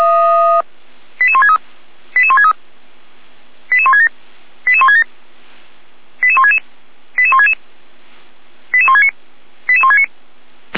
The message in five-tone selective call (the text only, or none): 88163 88163 88167 88167 88166 88166 88169 88169